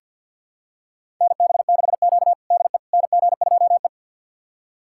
Morse code, heard as a keyed tone, about 50 wpm, 700 Hz. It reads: NB6Q DE NG2E
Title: USA Callsign DE Callsign - 4 Characters - 50wpm